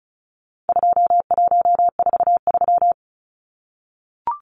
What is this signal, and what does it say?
Morse code, heard as a keyed tone, about 34 wpm, 700 Hz. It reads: 2143
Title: Numbers - 4-Digit 35wpm